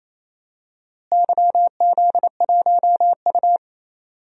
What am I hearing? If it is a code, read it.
Morse code, heard as a keyed tone, about 28 wpm, 700 Hz. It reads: YZ1U
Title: DXCC-100 Call Signs - 4 Characters - Repeated 3x - 28wpm